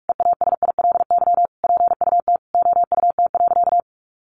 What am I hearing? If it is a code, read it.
Morse code, heard as a keyed tone, about 44 wpm, 700 Hz. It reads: EASILY PUT OUT.
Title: Sentences from Top 1400 Words 45wpm